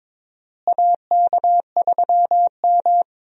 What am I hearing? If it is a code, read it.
Morse code, heard as a keyed tone, about 21 wpm, 700 Hz. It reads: AK3M
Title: USA Call Signs - 4 Characters - Repeated 2x - 22wpm